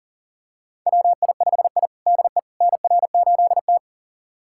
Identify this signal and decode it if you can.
Morse code, heard as a keyed tone, about 40 wpm, 700 Hz. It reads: WI5I DE NR8T
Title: USA Callsign DE Callsign - 4 Characters - 40wpm